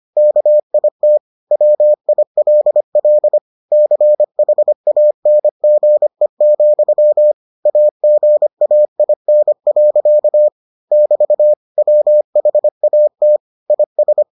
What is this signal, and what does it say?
Morse code, heard as a keyed tone, about 25 wpm, 600 Hz. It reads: KIT WILL CHANGE, AGAIN. <BT> WHAT IS